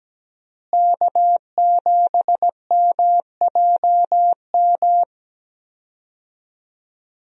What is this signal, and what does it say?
Morse code, heard as a keyed tone, about 17 wpm, 700 Hz. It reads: K7MJM
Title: USA Call Signs - 5 Characters - Repeated 3x - 17wpm